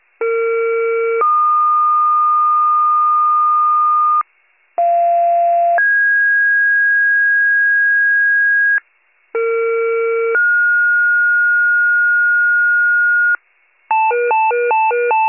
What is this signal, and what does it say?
Two-tone page, heard as a one-second tone, then a three-second tone, about 460 then 1220 Hz, about 690 then 1650 Hz, about 460 then 1430 Hz, with noise. It is heard from 0.2 to 13.4 s.